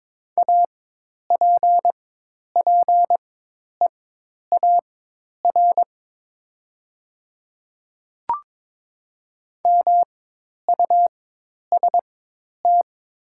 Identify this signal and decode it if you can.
Morse code, heard as a keyed tone, about 23 wpm, spaced as for 10 wpm, 700 Hz. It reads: APPEAR MUST
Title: 4x Character Spacing - Top 500 Words - 22wpm